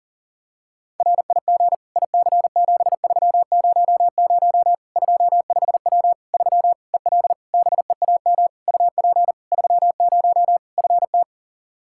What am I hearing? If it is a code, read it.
Morse code, heard as a keyed tone, about 40 wpm, 700 Hz. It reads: RIG IC7300 25W 3 EL BEAM UP 30 FT
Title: Practice QSO - Exchange 2 - Info - 40wpm